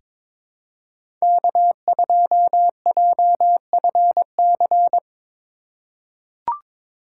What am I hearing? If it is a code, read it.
Morse code, heard as a keyed tone, about 21 wpm, 700 Hz. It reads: K2JFC
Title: USA Call Signs - 3x Character Spacing - 22wpm